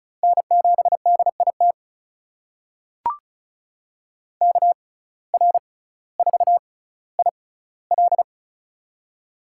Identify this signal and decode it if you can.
Morse code, heard as a keyed tone, about 36 wpm, spaced as for 18 wpm, 700 Hz. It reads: N7DIT KR4IL
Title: USA Call Signs - 5 Characters - 6x Character Spacing - 35wpm